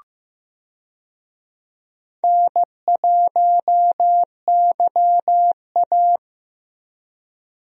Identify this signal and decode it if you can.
Morse code, heard as a keyed tone, about 15 wpm, 700 Hz. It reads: N1YA
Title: USA Call Signs - 4 Characters - Repeated 3x - 15wpm